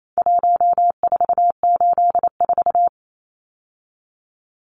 Morse code, heard as a keyed tone, about 28 wpm, 700 Hz. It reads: 1484
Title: Numbers - Contest Style - 4-Digit 28wpm